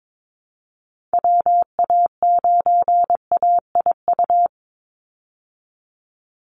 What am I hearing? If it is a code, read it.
Morse code, heard as a keyed tone, about 22 wpm, 700 Hz. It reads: WA9AIU
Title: USA Call Signs 22wpm